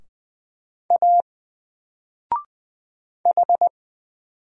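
Morse code, heard as a keyed tone, about 20 wpm, 700 Hz. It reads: A H
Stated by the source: Single letter and number Morse drill at 20 wpm, from N4LCD.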